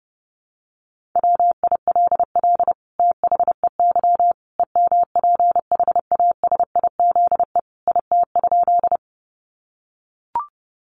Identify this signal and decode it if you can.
Morse code, heard as a keyed tone, about 28 wpm, 700 Hz. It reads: WILL THEY EMPHASIZE IT?
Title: Sentences from Top 1500 Words 30wpm